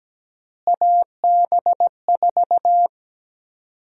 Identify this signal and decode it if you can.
Morse code, heard as a keyed tone, about 17 wpm, 700 Hz. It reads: AB4
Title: USA Call Sign Prefixes - Repeated 2x - 17wpm